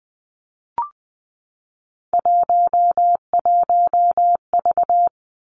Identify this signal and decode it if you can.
Morse code, heard as a keyed tone, about 19 wpm, 700 Hz. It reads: 11V
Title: Three Challenging Letter-Number Combinations 20wpm